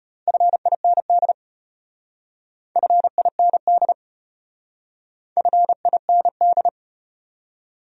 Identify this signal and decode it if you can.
Morse code, heard as a keyed tone, about 34 wpm, 700 Hz. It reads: FIND FIND FIND